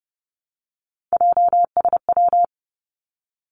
Morse code, heard as a keyed tone, about 30 wpm, 700 Hz. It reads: JSW
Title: Three Challenging Letter-Number Combinations 30wpm